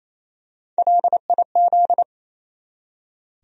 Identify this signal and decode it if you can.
Morse code, heard as a keyed tone, about 28 wpm, 700 Hz. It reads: LIZ
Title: USA Common Names - Max 3 Letters - 28wpm